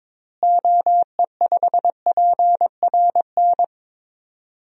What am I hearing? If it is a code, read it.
Morse code, heard as a keyed tone, about 22 wpm, 700 Hz. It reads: OE5PRN
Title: DXCC-100 Call Signs - 6 Characters - Repeated 3x - 22wpm